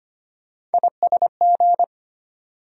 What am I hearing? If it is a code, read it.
Morse code, heard as a keyed tone, about 25 wpm, 700 Hz. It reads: ISG
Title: USA Call Sign Postfixes - 3 Letters 25wpm